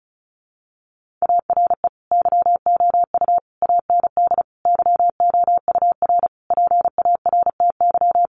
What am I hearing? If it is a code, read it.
Morse code, heard as a keyed tone, about 34 wpm, 700 Hz. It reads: ARE YOU AND YOUR PARTY
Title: Sentences from Top 1400 Words 35wpm